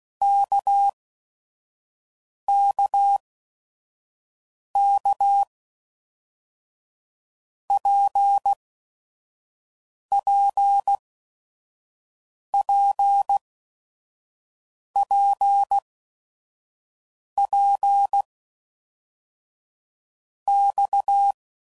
Morse code, heard as a keyed tone, about 16 wpm, 800 Hz. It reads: K K K P P P P P X
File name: Basic Letter Groups By7Groups / RKPX / RKPX@5 f15